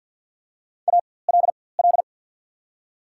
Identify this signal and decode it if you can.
Morse code, heard as a keyed tone, about 53 wpm, spaced as for 24 wpm, 690 Hz. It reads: ALL
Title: 4x Character Spacing - Top 500 Words - 50wpm